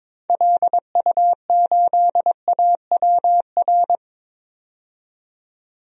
Morse code, heard as a keyed tone, about 22 wpm, 700 Hz. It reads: LU8AWR
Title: DXCC-100 Call Signs - 6 Characters - 22wpm